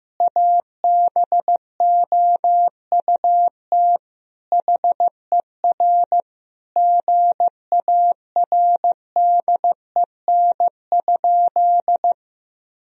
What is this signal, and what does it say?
Morse code, heard as a keyed tone, about 15 wpm, 700 Hz. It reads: ABOUT HER GARDEN?
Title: Sentences from Top 1000 Words - Review Part 4 - 15wpm